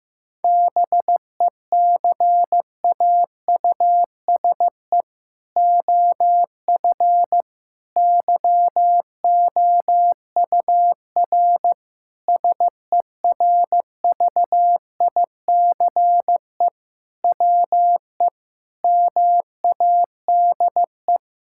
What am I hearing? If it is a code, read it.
Morse code, heard as a keyed tone, about 15 wpm, 700 Hz. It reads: BECAUSE OF YOUR SERVICE WE MADE
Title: Sentences from Top 1000 Words - Review Part 5 - 15wpm